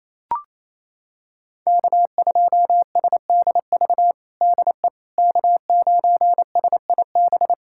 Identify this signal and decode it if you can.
Morse code, heard as a keyed tone, about 28 wpm, 700 Hz. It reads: K2SDV DE K9SIB
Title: USA Callsign DE Callsign - 5 Characters - 28wpm